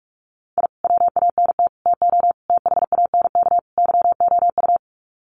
Morse code, heard as a keyed tone, about 44 wpm, 700 Hz. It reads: I WANT TO THANK YOU
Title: Sets of 5 Words Encore 45wpm